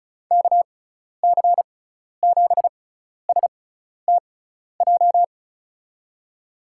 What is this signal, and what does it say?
Morse code, heard as a keyed tone, about 34 wpm, spaced as for 12 wpm, 700 Hz. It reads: KC7STJ
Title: USA Call Signs - 6x Character Spacing - 35wpm